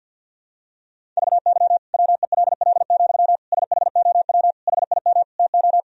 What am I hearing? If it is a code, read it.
Morse code, heard as a keyed tone, about 47 wpm, 700 Hz. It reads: VY WELL, SHOW HIM TO
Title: Sentences from Top 200 Words and Abbreviations 50wpm